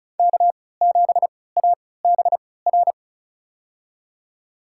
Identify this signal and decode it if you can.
Morse code, heard as a keyed tone, about 34 wpm, spaced as for 20 wpm, 700 Hz. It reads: K7ABR
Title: USA Call Signs - 5 Characters - 3x Character Spacing - 35wpm